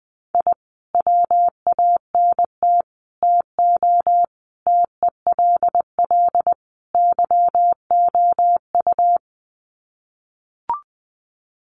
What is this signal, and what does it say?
Morse code, heard as a keyed tone, about 20 wpm, 700 Hz. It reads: I WANT TO TELL YOU
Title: Sets of 5 Words Encore 20wpm